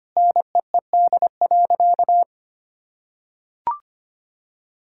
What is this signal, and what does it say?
Morse code, heard as a keyed tone, about 25 wpm, 700 Hz. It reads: NEED.
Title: Sentences from Top 1000 Words - Review Part 6 - 25wpm